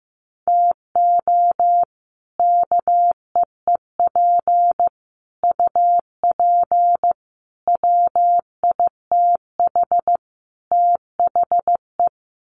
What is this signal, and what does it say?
Morse code, heard as a keyed tone, about 15 wpm, 700 Hz. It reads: TO KEEP UP WITH THE
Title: Sets of 5 Words Encore 15wpm